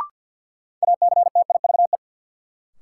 Morse code, heard as a keyed tone, about 53 wpm, 700 Hz. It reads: ACTIVE
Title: Sentences from Top 900 Words - New Word Drill - 50wpm